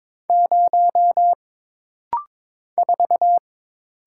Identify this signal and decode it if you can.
Morse code, heard as a keyed tone, about 22 wpm, 700 Hz. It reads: 0 4
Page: This rapid-fire timing is geared to push your ICR (Instant Character Recognition) skills.